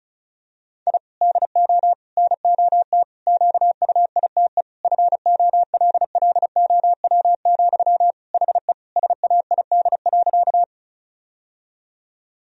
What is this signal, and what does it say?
Morse code, heard as a keyed tone, about 34 wpm, 700 Hz. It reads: I DO NOT QUITE FOLLOW, HE SAID.
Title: Sentences from Top 1000 Words - Review Part 1 - 35wpm